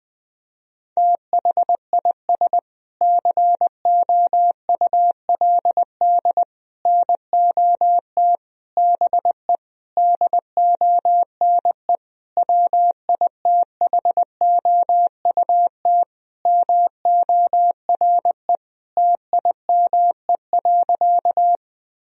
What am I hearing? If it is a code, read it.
Morse code, heard as a keyed tone, about 20 wpm, 700 Hz. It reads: THIS COULD NOT BE DONE WITHOUT MORE TIME.
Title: Sentences from Top 1000 Words - Review Part 3 - 20wpm